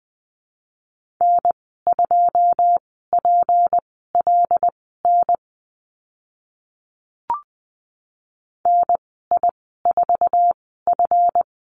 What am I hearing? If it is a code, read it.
Morse code, heard as a keyed tone, about 20 wpm, spaced as for 14 wpm, 700 Hz. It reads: N2PLN NI4F
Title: 161 Lesson 26g - Letter g - Call Signs 20wpm